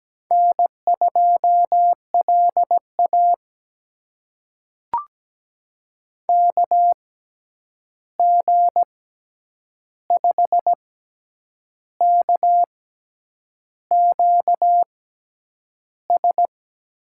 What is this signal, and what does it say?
Morse code, heard as a keyed tone, about 17 wpm, spaced as for 8 wpm, 700 Hz. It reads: N2LA KG5KQS
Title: USA Call Signs - 6x Character Spacing - 17wpm